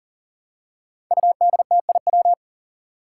Morse code, heard as a keyed tone, about 37 wpm, 700 Hz. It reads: UDTIW
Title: Five Challenging Letter-Number Combinations 40wpm